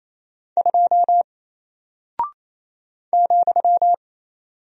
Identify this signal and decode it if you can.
Morse code, heard as a keyed tone, about 28 wpm, 700 Hz. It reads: 2 ,
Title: Single Character (Letter:Number:Punctuation:Prosign) Alt 28wpm